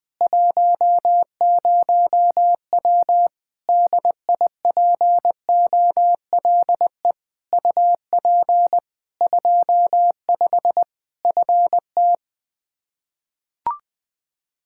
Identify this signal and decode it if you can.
Morse code, heard as a keyed tone, about 20 wpm, 700 Hz. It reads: 10W DIPOLE UP 25 FT
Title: K1USN SST QSO Practice 20wpm